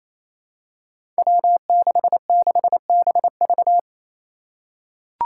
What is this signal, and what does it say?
Morse code, heard as a keyed tone, about 28 wpm, 700 Hz. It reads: W66BV
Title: Five Challenging Letter-Number Combinations 28wpm